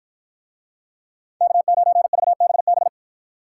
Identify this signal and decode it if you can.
Morse code, heard as a keyed tone, about 48 wpm, 700 Hz. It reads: K9VBB